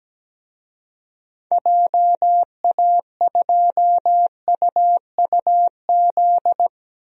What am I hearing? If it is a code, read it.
Morse code, heard as a keyed tone, about 17 wpm, 700 Hz. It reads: JA2UUZ
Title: DXCC-100 Call Signs - 6 Characters - Repeated 3x - 17wpm